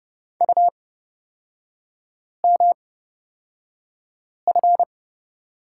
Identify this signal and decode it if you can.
Morse code, heard as a keyed tone, about 30 wpm, spaced as for 5 wpm, 700 Hz. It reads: UMF
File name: Lesson 36e - BT - Three Characters 30wpm